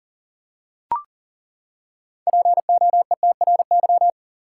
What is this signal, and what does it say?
Morse code, heard as a keyed tone, about 37 wpm, 700 Hz. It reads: POETRY
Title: Six Letter Words - 40wpm